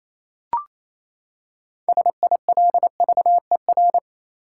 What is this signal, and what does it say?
Morse code, heard as a keyed tone, about 28 wpm, 700 Hz. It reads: SILVER
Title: Six Letter Words - 28wpm